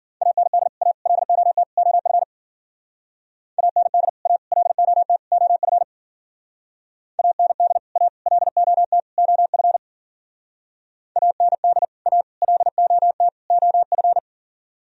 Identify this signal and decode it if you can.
Morse code, heard as a keyed tone, about 50 wpm, 700 Hz. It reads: AND A LOT OF AND A LOT OF AND A LOT OF AND A LOT OF
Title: Speed-Racing - Sets of 4 Words 40wpm